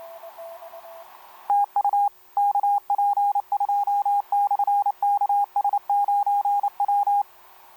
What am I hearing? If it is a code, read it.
Morse code, heard as a keyed tone, about 28 wpm, 810 Hz, with radio noise. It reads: TU KP2/KS9W
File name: kp2-ks9w sm1cq etc 12m cw 13mar2011 1339